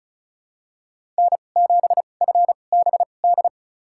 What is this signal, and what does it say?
Morse code, heard as a keyed tone, about 34 wpm, spaced as for 25 wpm, 700 Hz. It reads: N7FBD
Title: USA Call Signs - 5 Characters - 2x Character Spacing - 35wpm